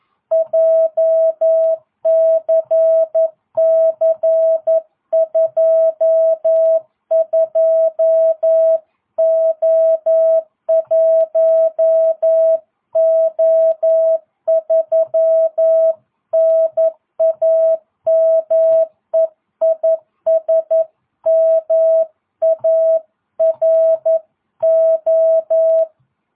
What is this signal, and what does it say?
Morse code, heard as a keyed tone, about 11 wpm, 650 Hz. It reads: JCC22O1O3 NAME IS MARO